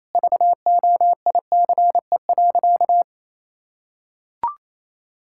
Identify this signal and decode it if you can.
Morse code, heard as a keyed tone, about 28 wpm, 700 Hz. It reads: VOICE.
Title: Sentences from Top 1600 Words 28wpm